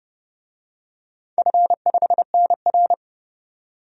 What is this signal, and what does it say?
Morse code, heard as a keyed tone, about 30 wpm, 700 Hz. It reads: F5NR
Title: DXCC-100 Call Signs - 4 Characters - 30wpm